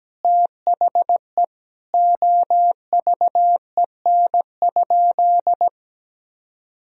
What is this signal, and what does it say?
Morse code, heard as a keyed tone, about 17 wpm, 700 Hz. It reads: THE OVEN?
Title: Sentences from Top 1000 Words - Review Part 4 - 17wpm